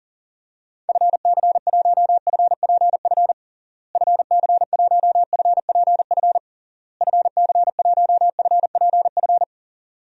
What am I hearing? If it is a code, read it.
Morse code, heard as a keyed tone, about 40 wpm, 700 Hz. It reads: FC1FPF FC1FPF FC1FPF
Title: DXCC-100 Call Signs - 6 Characters - Repeated 3x - 40wpm